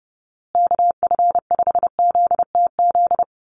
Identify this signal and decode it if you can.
Morse code, heard as a keyed tone, about 30 wpm, 700 Hz. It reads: KF5ZTZ
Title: USA Call Signs Repeated 3x 30wpm